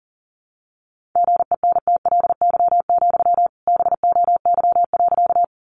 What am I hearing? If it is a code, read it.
Morse code, heard as a keyed tone, about 37 wpm, 700 Hz. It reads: GENTLY, BOY.
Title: Sentences from Top 1500 Words 40wpm